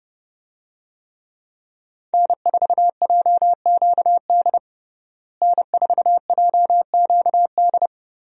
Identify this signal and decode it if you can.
Morse code, heard as a keyed tone, about 32 wpm, 700 Hz. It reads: N4JQD N4JQD
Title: USA Call Signs - 5 Characters - Repeated 3x - 30wpm